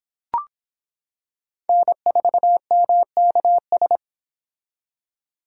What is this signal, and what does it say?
Morse code, heard as a keyed tone, about 26 wpm, 700 Hz. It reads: N4MKS